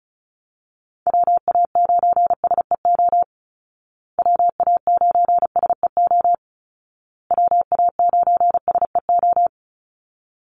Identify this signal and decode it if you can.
Morse code, heard as a keyed tone, about 34 wpm, 700 Hz. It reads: WA9SEO WA9SEO WA9SEO
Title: Call Signs Repeated 2x - 35wpm